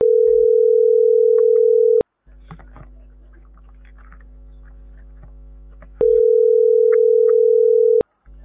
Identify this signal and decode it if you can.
Ringback tone (North America), 440 + 480 Hz.